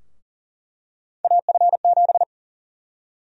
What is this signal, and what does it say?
Morse code, heard as a keyed tone, about 40 wpm, 700 Hz. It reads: AF7
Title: USA Call Sign Prefixes - 40wpm